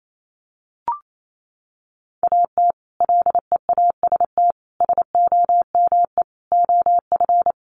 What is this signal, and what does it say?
Morse code, heard as a keyed tone, about 28 wpm, 700 Hz. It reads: AT LEAST SOME OF
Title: Sets of 5 Words Encore 28wpm